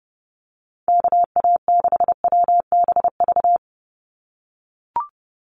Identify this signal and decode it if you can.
Morse code, heard as a keyed tone, about 30 wpm, 700 Hz. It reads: KA6WBV
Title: USA Call Signs 30wpm